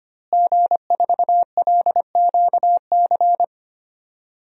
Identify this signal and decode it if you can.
Morse code, heard as a keyed tone, about 25 wpm, 700 Hz. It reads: G4LQC
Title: DXCC-100 Call Signs - 5 Characters - Repeated 2x - 25wpm